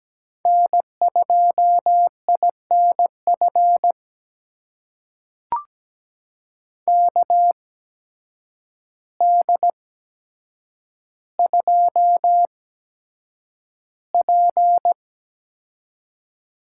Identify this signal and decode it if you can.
Morse code, heard as a keyed tone, about 17 wpm, spaced as for 8 wpm, 700 Hz. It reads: N2INF KD2P
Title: USA Call Signs - 8x Character Spacing - 17wpm